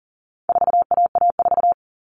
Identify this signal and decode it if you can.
Morse code, heard as a keyed tone, about 40 wpm, 700 Hz. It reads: 4AA4
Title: All Cut Numbers - 4-Digit 40wpm